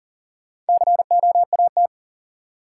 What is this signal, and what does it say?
Morse code, heard as a keyed tone, about 40 wpm, 700 Hz. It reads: COAT
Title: Four Letter Words - 40wpm